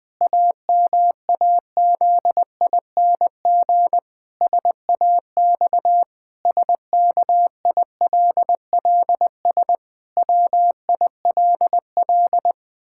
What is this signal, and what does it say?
Morse code, heard as a keyed tone, about 20 wpm, 700 Hz. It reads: AMAZING SAX SKILLS WILL
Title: Pangram Challenge v2 - 20wpm